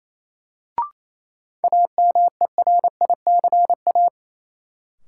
Morse code, heard as a keyed tone, about 28 wpm, 700 Hz. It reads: AMERICA
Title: Sets of 5 Words - New Word Drill - 28wpm